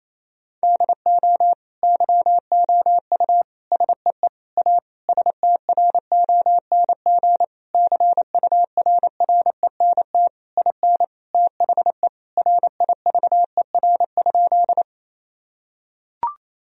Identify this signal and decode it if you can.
Morse code, heard as a keyed tone, about 28 wpm, 700 Hz. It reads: DO YOU SEE A STRONG CURRENT IN THE RIVER?
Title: Sentences from Top 1000 Words - Review Part 4 - 28wpm